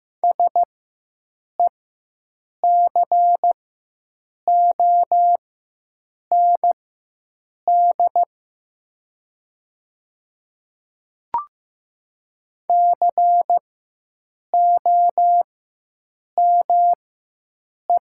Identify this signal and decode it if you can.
Morse code, heard as a keyed tone, about 15 wpm, spaced as for 7 wpm, 700 Hz. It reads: SECOND COME
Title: 4x Character Spacing - Top 300 Words - 15wpm